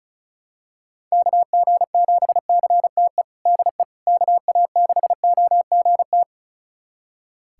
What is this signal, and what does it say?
Morse code, heard as a keyed tone, about 36 wpm, 700 Hz. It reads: KG7CTE DE KA6OGT
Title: USA Callsign DE Callsign - 6 Characters - 35wpm